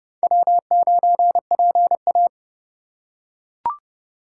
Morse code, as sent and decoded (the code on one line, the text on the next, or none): .-- ----. .--. .-
W9PA